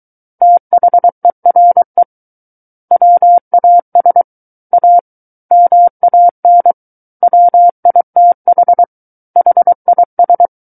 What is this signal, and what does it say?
Morse code, heard as a keyed tone, about 23 wpm, 700 Hz. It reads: THERE WAS A MAN WITH HIS
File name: Book One / 14-Chapter 14 - 23 WPM